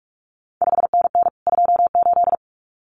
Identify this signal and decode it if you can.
Morse code, heard as a keyed tone, about 46 wpm, 700 Hz. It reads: 5NN 28
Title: CQ WW DX Contest Exchange Encore - 45wpm